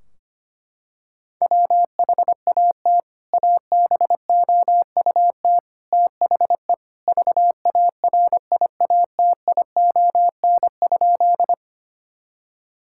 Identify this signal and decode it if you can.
Morse code, heard as a keyed tone, about 25 wpm, 700 Hz. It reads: WHAT ABOUT THE VARIATION?
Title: Sentences from Top 1600 Words 25wpm